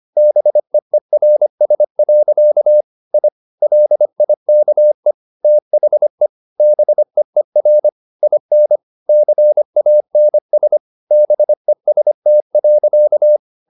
Morse code, heard as a keyed tone, about 25 wpm, 600 Hz. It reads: BEERS. I LIKE THE BEER IN CANS BEST.
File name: the old man and the sea part 1 25wpm